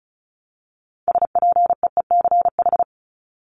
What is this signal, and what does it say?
Morse code, heard as a keyed tone, about 34 wpm, 700 Hz. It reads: SPEECH
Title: Top 1751-2000 Words 35wpm